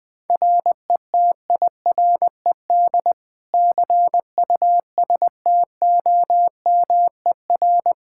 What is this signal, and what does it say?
Morse code, heard as a keyed tone, about 20 wpm, 700 Hz. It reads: RETIRED CUSTOMER
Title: Practice QSO - Exchange 3 - More Info - 20wpm